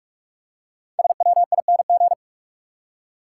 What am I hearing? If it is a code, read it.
Morse code, heard as a keyed tone, about 45 wpm, 700 Hz. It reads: SWING
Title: Five Letter Words - 45wpm